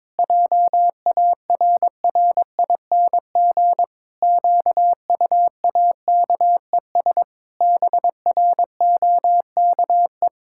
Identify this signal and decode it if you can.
Morse code, heard as a keyed tone, about 22 wpm, 700 Hz. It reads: JARRING QUAKES BROKE
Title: Pangram Challenge v2 - 22wpm